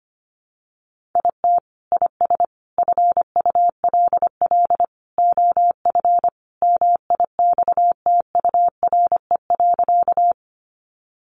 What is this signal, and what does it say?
Morse code, heard as a keyed tone, about 25 wpm, 700 Hz. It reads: IT IS FULL OF MIXTURE.
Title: Sentences from Top 1500 Words 25wpm